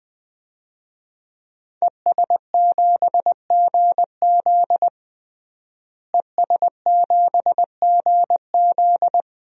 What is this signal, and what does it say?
Morse code, heard as a keyed tone, about 21 wpm, 700 Hz. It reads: ES7GZ ES7GZ
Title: DXCC-100 Call Signs - 5 Characters - Repeated 2x - 20wpm